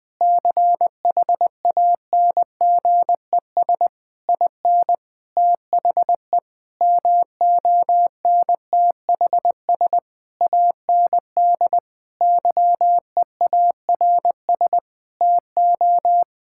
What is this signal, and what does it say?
Morse code, heard as a keyed tone, about 21 wpm, 700 Hz. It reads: CHANGES IN THE MONTHS AND YEARS TO
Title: Sentences with 7-Word Phrases - 20wpm